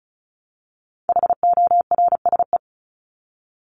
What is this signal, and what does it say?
Morse code, heard as a keyed tone, about 34 wpm, 700 Hz. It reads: HORSE
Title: Top 500 Words Repeated 35wpm